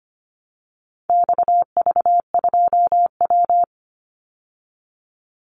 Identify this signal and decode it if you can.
Morse code, heard as a keyed tone, about 25 wpm, 700 Hz. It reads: XV2W
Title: Call Signs Repeated 3x - 25wpm